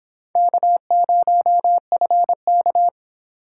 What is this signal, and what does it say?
Morse code, heard as a keyed tone, about 26 wpm, 700 Hz. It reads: K0FK